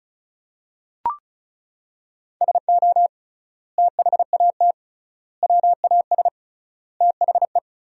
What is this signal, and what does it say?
Morse code, heard as a keyed tone, about 34 wpm, 700 Hz. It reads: SO THAT WAS THE
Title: 3x Word-Spacing - Sentences from Top 100 Words - 35wpm